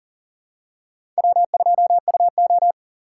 Morse code, heard as a keyed tone, about 40 wpm, 700 Hz. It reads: W2UO
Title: USA Call Signs - 4 Characters - 6x Character Spacing - 40wpm